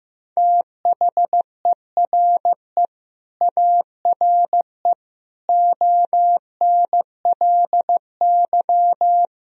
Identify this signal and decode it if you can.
Morse code, heard as a keyed tone, about 15 wpm, 700 Hz. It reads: THERE ARE ONLY
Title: Sentences from Top 1000 Words - Review Part 1 - 15wpm